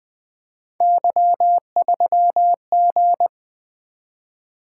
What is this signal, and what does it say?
Morse code, heard as a keyed tone, about 21 wpm, 700 Hz. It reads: Y3G
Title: Three Letter-Number Combinations - 20wpm